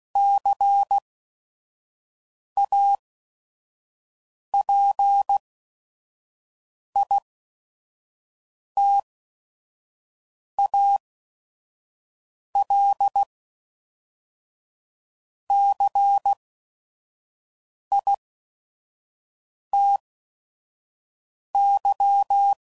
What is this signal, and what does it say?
Morse code, heard as a keyed tone, about 16 wpm, spaced as for 5 wpm, 800 Hz. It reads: CAPITAL CITY
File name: CW Code Groups @5WPM / fromSM's / Cities